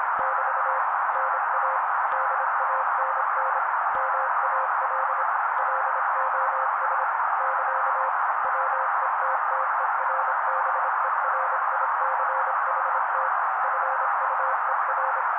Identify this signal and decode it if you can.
Morse code, heard as a keyed tone, about 26 wpm, 530 Hz. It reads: <BT> NA DANN MAL LOS <KA> WETTERBERICHT FUER
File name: daily / 25wpm noise